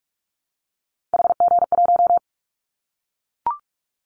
Morse code, heard as a keyed tone, about 44 wpm, 700 Hz. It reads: HG1
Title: Callsign Prefixes for Top 25 DXCC - Repeated 3x - 45wpm